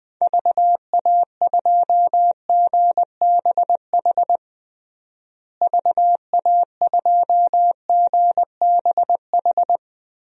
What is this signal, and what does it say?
Morse code, heard as a keyed tone, about 20 wpm, 700 Hz. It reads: VA2GBH VA2GBH
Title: DXCC-100 Call Signs - 6 Characters - Repeated 3x - 20wpm